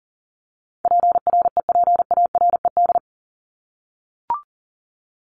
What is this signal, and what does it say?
Morse code, heard as a keyed tone, about 40 wpm, 700 Hz. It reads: PREPARED
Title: ED Suffix - Top 100 Words with Base Words 40wpm